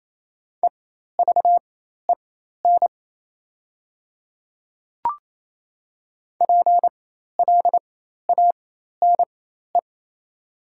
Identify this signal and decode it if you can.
Morse code, heard as a keyed tone, about 28 wpm, spaced as for 13 wpm, 700 Hz. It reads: EVEN PLANE
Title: 4x Character Spacing - Top 500 Words - 28wpm